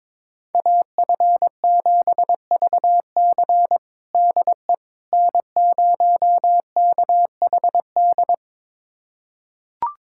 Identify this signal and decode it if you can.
Morse code, heard as a keyed tone, about 21 wpm, 700 Hz. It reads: AF7VC DE N0KHD
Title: USA Callsign DE Callsign - 5 Characters - 22wpm